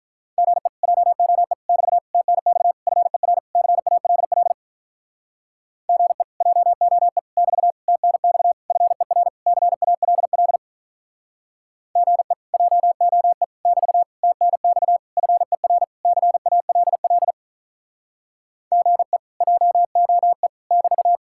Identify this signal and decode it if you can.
Morse code, heard as a keyed tone, about 47 wpm, 700 Hz. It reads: GE JOE <BT> TNX FER CALL GE JOE <BT> TNX FER CALL GE JOE <BT> TNX FER CALL GE JOE <BT>